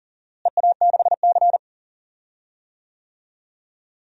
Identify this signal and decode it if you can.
Morse code, heard as a keyed tone, about 40 wpm, 700 Hz. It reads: EA6C
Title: DXCC-100 Call Signs - 4 Characters - Repeated 2x - 40wpm